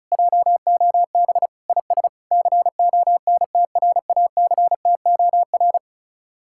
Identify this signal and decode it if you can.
Morse code, heard as a keyed tone, about 45 wpm, 700 Hz. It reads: JOB IS CONTRACTOR